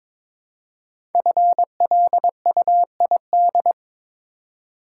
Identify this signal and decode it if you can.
Morse code, heard as a keyed tone, about 23 wpm, 700 Hz. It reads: FLUID
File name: 5 Letter Words - First 3 Letters Given - 22wpm